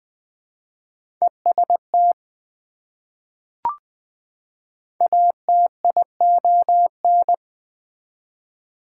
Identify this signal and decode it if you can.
Morse code, heard as a keyed tone, about 20 wpm, 700 Hz. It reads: EST ATION
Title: Prefixes and Suffixes 20wpm